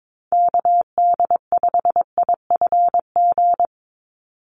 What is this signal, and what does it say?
Morse code, heard as a keyed tone, about 22 wpm, 700 Hz. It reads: KD5IFG
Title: USA Call Signs 22wpm